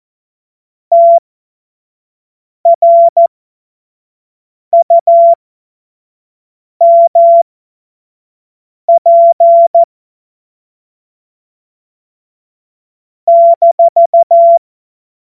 Morse code, heard as a keyed tone, about 14 wpm, spaced as for 5 wpm, 680 Hz. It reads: TRUMP -
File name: Morsekurs / Witze / Witz 16. Februar 2021, 25 F70